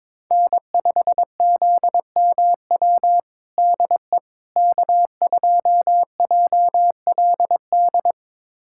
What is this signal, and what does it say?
Morse code, heard as a keyed tone, about 22 wpm, 700 Hz. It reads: N5ZMW DE K2JLD
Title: USA Callsign DE Callsign - 5 Characters - 22wpm